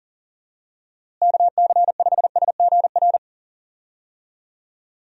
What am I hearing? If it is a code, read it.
Morse code, heard as a keyed tone, about 37 wpm, 700 Hz. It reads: KC5SGR
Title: USA Call Signs - 6 Characters - Repeated 3x - 40wpm